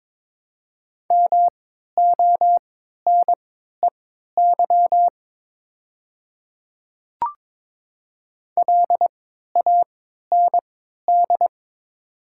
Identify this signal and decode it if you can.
Morse code, heard as a keyed tone, about 22 wpm, spaced as for 12 wpm, 700 Hz. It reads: MONEY LAND
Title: 3x Character Spacing - Top 500 Words - 22wpm